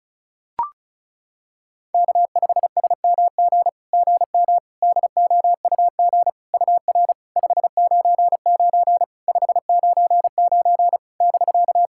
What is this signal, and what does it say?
Morse code, heard as a keyed tone, about 36 wpm, 700 Hz. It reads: K5SMG GM DOUG UR 599 599 <BK>
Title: SOTA QSO Practice 35wpm